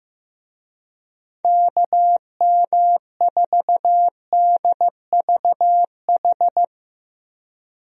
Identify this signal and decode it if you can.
Morse code, heard as a keyed tone, about 15 wpm, 700 Hz. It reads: KM4DVH
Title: USA Call Signs - 6 Characters - Repeated 3x - 15wpm